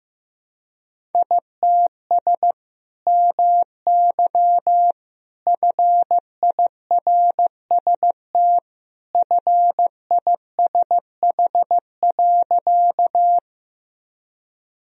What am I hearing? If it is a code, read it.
Morse code, heard as a keyed tone, about 15 wpm, 700 Hz. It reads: ITS MY FIRST FISH.
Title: Sentences from Top 1000 Words - Review Part 1 - 15wpm